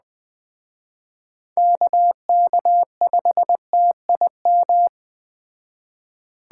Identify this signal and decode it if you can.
Morse code, heard as a keyed tone, about 20 wpm, 700 Hz. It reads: KK5TIM
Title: Speed Racing - USA 2x3 Call Signs 17wpm